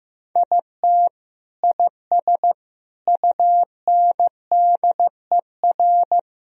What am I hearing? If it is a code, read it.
Morse code, heard as a keyed tone, about 16 wpm, 700 Hz. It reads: IT IS UNDER
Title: Sentences from Top 1000 Words - Review Part 1 - 15wpm